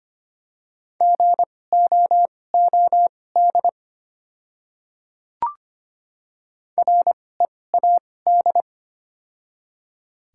Morse code, heard as a keyed tone, about 25 wpm, spaced as for 18 wpm, 700 Hz. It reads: GOOD READ
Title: 2x Character Spacing - Top 200 Words - 25wpm